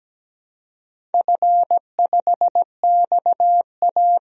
Morse code, heard as a keyed tone, about 17 wpm, 700 Hz. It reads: F5XA
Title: DXCC-100 Call Signs - 4 Characters - 17wpm